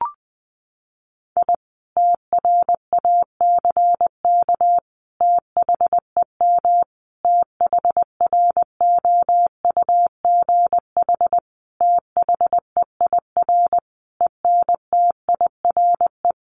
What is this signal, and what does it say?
Morse code, heard as a keyed tone, about 20 wpm, 700 Hz. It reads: I TRACK THEM THROUGH THEIR ENTIRE
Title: A Sound Of Thunder 20wpm